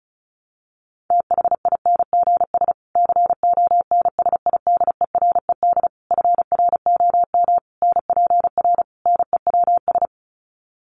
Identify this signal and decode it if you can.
Morse code, heard as a keyed tone, about 34 wpm, 700 Hz. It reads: THINGS CONSIDERED FROM NPR NEWS
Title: Sets of 5 Words Encore 35wpm